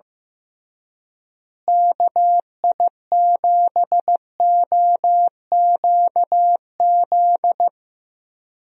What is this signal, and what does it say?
Morse code, heard as a keyed tone, about 15 wpm, 700 Hz. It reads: KI7OQZ
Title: USA Call Signs - 6 Characters - Repeated 3x - 15wpm